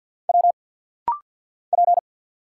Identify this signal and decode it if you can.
Morse code, heard as a keyed tone, about 50 wpm, 690 Hz. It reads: W P
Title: Single Letters - Rapid-Fire 50wpm